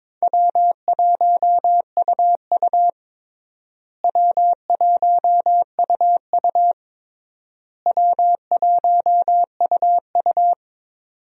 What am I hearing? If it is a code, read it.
Morse code, heard as a keyed tone, about 22 wpm, 700 Hz. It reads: W1UU W1UU W1UU
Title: USA Call Signs - 4 Characters - Repeated 3x - 22wpm